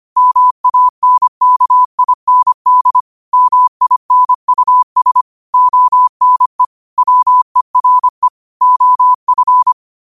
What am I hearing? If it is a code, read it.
Morse code, heard as a keyed tone, about 25 wpm, 1000 Hz. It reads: MANKIND MINUS ONE WERE OF